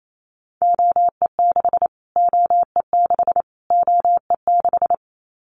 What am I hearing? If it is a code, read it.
Morse code, heard as a keyed tone, about 28 wpm, 700 Hz. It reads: OE6 OE6 OE6
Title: Callsign Prefixes for Top 25 DXCC - Repeated 3x - 28wpm